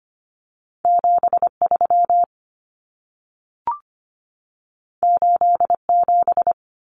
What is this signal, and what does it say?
Morse code, heard as a keyed tone, about 25 wpm, 700 Hz. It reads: 73 87
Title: Common Cut Numbers - 2-Digit 25wpm